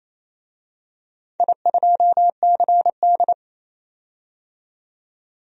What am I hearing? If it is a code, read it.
Morse code, heard as a keyed tone, about 28 wpm, 700 Hz. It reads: I2CD
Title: DXCC-100 Call Signs - 4 Characters - 28wpm